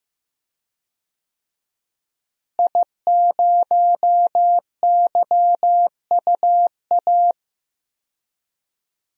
Morse code, heard as a keyed tone, about 15 wpm, 700 Hz. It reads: I0YUA
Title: DXCC-100 Call Signs - 5 Characters - Repeated 3x - 15wpm